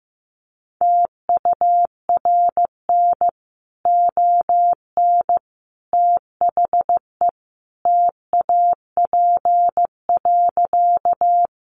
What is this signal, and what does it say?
Morse code, heard as a keyed tone, about 15 wpm, 700 Hz. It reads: TURN ON THE TAP.
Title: Sentences from Top 1400 Words 15wpm